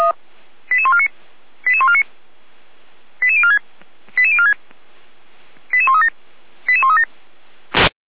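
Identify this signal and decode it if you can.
Five-tone selective call (ZVEI), heard as quick runs of five tones, tones of about 70 ms. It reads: 88149 88149 88046 88046 88137 88137
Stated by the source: Five-tone ZVEI selective call received on an SDR.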